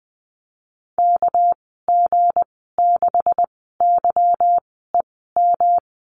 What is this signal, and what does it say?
Morse code, heard as a keyed tone, about 20 wpm, spaced as for 15 wpm, 700 Hz. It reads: KG6YEM
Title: 200 Lesson 32g - Letter k - Call Signs 20wpm